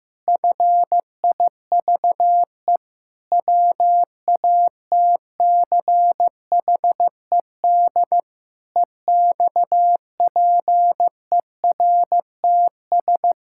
Morse code, heard as a keyed tone, about 15 wpm, 700 Hz. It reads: FIVE WATCHED EXPERTS
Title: Pangram Challenge v2 - 15wpm